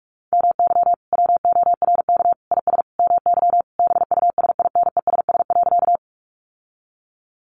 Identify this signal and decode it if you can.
Morse code, heard as a keyed tone, about 44 wpm, 700 Hz. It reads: MY WORK IS MY BUSINESS.
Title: Sentences from Top 600 Words 45wpm